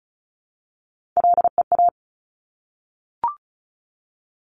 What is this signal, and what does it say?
Morse code, heard as a keyed tone, about 36 wpm, 700 Hz. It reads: LEA
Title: Top Three Letter Combinations in Words 35wpm